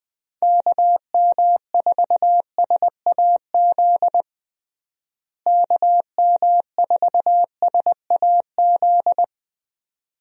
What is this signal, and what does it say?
Morse code, heard as a keyed tone, about 20 wpm, 700 Hz. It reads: KM4SAZ KM4SAZ
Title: USA Call Signs - 6 Characters - Repeated 2x - 20wpm